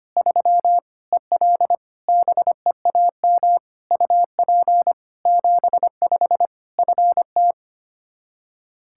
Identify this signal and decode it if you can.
Morse code, heard as a keyed tone, about 25 wpm, 700 Hz. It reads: 3 EL BEAM UP 75 FT
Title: K1USN SST QSO Practice 25wpm